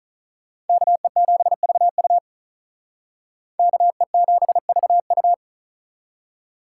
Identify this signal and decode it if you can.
Morse code, heard as a keyed tone, about 46 wpm, 700 Hz. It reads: KE7VU KE7VU